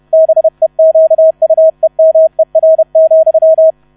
Morse code, heard as a keyed tone, about 31 wpm, 640 Hz, with radio noise. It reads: BEQUEMER,
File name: 30wpm